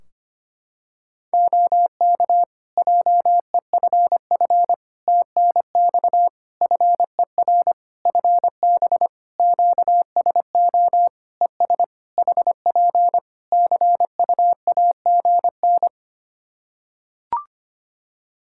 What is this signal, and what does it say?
Morse code, heard as a keyed tone, about 25 wpm, 700 Hz. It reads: OK JEFF TNX FER FB QSO ES HP CUAGN
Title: Practice QSO - Exchange 4 - Ending - 25wpm